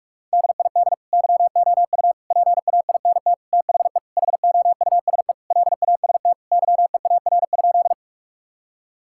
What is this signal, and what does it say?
Morse code, heard as a keyed tone, about 46 wpm, 700 Hz. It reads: DID YOU PAINT THE HOUSE LAST YEAR?
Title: Sentences from Top 1000 Words - Review Part 6 - 45wpm